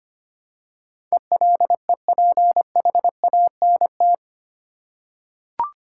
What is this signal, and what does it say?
Morse code, heard as a keyed tone, about 25 wpm, 700 Hz. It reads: ELEPHANT
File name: 8 Letter Words - First 3 Letters Given - 25wpm